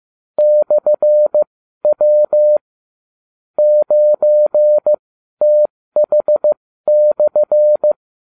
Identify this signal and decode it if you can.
Morse code, heard as a keyed tone, about 15 wpm, spaced as for 12 wpm, 600 Hz. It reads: /W 9TH/